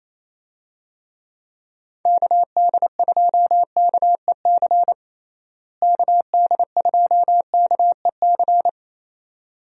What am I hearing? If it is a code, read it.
Morse code, heard as a keyed tone, about 28 wpm, 700 Hz. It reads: KD2KEC KD2KEC
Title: USA Call Signs - 6 Characters - Repeated 2x - 28wpm